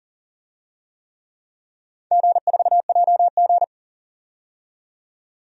Morse code, heard as a keyed tone, about 40 wpm, 700 Hz. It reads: G4JG
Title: DXCC-100 Call Signs - 4 Characters - 40wpm